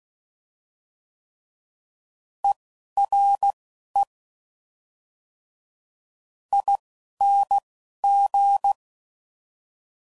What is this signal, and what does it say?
Morse code, heard as a keyed tone, about 16 wpm, spaced as for 11 wpm, 800 Hz. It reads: ERE ING
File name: Suffixes / General / ERE INGs